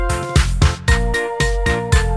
There is no code.